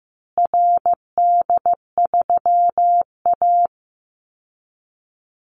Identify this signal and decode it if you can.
Morse code, heard as a keyed tone, about 15 wpm, 700 Hz. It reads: RD3A
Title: Call Signs Repeated 2x - 15wpm